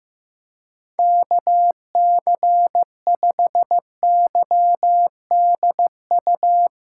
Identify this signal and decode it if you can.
Morse code, heard as a keyed tone, about 16 wpm, 700 Hz. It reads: KC5YDU
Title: Speed Racing - USA 2x3 Call Signs 15wpm